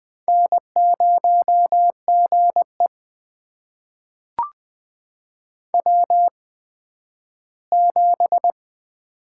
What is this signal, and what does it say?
Morse code, heard as a keyed tone, about 20 wpm, spaced as for 12 wpm, 700 Hz. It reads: N0GE W7
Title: USA Call Signs - 8x Character Spacing - 20wpm